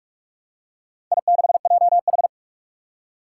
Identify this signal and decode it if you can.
Morse code, heard as a keyed tone, about 44 wpm, 700 Hz. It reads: I6JH
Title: Four Challenging Letter-Number Combinations 45wpm